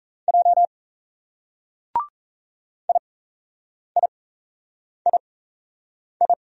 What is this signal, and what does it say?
Morse code, heard as a keyed tone, about 32 wpm, spaced as for 8 wpm, 700 Hz. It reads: J IIII
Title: Speed-Racing Single Letters 28wpm